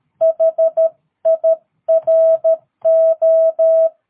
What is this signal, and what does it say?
Morse code, heard as a keyed tone, about 13 wpm, 650 Hz. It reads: HIRO
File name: jh8cbh / 2022cw